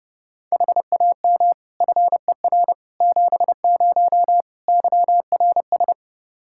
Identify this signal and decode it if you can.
Morse code, heard as a keyed tone, about 32 wpm, 700 Hz. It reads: HAM FER 70 YRS
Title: Practice QSO - Exchange 3 - More Info - 30wpm